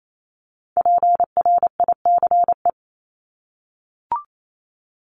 Morse code, heard as a keyed tone, about 28 wpm, 700 Hz. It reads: PRICE
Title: Top 501-750 Words 28wpm